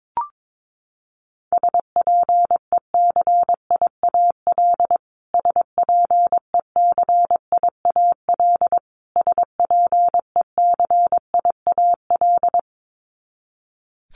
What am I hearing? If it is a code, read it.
Morse code, heard as a keyed tone, about 22 wpm, 700 Hz. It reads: SPECIAL SPECIAL SPECIAL
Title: Top 500 Words Repeated 22wpm